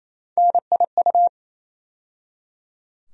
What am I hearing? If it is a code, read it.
Morse code, heard as a keyed tone, about 27 wpm, 700 Hz. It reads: NIU
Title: USA Call Sign Postfixes - 3 Letters 28wpm